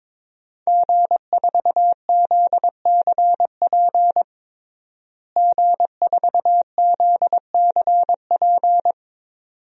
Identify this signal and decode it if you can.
Morse code, heard as a keyed tone, about 22 wpm, 700 Hz. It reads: G4ZCP G4ZCP
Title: DXCC-100 Call Signs - 5 Characters - Repeated 2x - 22wpm